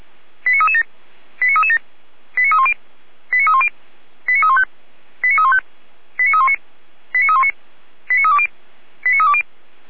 Five-tone selective call (ZVEI), heard as quick runs of five tones, tones of about 70 ms. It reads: 89337 89337 89310 89310 89315 89315 89319 89319 89320 89320
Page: /455/SDR/ZVEI/